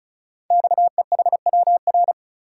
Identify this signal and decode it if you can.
Morse code, heard as a keyed tone, about 36 wpm, 700 Hz. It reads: XEHWR
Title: Code Groups - 5 Letters - 35wpm